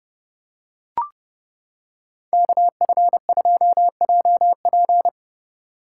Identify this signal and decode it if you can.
Morse code, heard as a keyed tone, about 29 wpm, 700 Hz. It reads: KF2JP